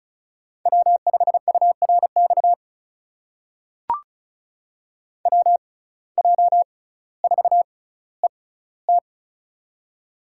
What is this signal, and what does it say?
Morse code, heard as a keyed tone, about 36 wpm, spaced as for 18 wpm, 700 Hz. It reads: W5URX WJ4ET
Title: USA Call Signs - 5 Characters - 6x Character Spacing - 35wpm